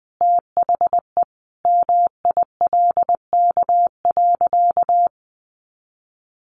Morse code, heard as a keyed tone, about 19 wpm, 700 Hz. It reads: THE MILK.
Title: Additional Easy Phrases 20wpm